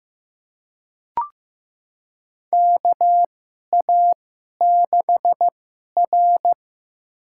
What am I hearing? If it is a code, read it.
Morse code, heard as a keyed tone, about 15 wpm, spaced as for 11 wpm, 700 Hz. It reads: KA6R
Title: USA Call Signs - 2x Character Spacing - 15wpm